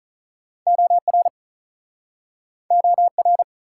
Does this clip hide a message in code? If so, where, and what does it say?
Yes: Morse code from 0.7 to 3.4 s, heard as a keyed tone, about 48 wpm, 700 Hz. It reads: OR OR